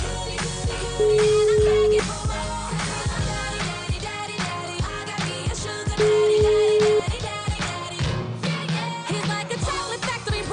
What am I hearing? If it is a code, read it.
Ringback tone (Europe, 425 Hz), with noise.